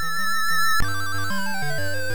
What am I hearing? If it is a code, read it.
There is no code.